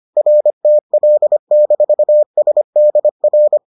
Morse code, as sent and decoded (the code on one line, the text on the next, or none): .-. - .-.. -....- ... -.. .-.
RTL-SDR